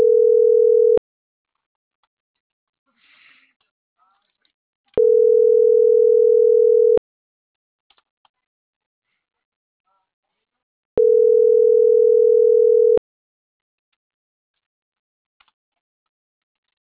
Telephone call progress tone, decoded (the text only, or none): ringback tone